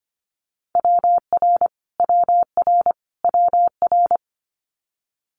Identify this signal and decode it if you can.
Morse code, heard as a keyed tone, about 25 wpm, 700 Hz. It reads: WR WR WR
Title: Multi-letter Phonemes Repeated 3x 25wpm